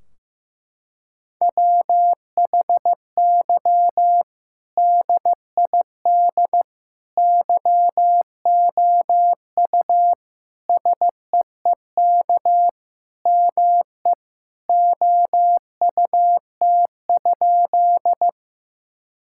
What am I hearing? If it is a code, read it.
Morse code, heard as a keyed tone, about 15 wpm, 700 Hz. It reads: WHY DID YOU SEEK ME OUT?
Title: Sentences from Top 1600 Words 15wpm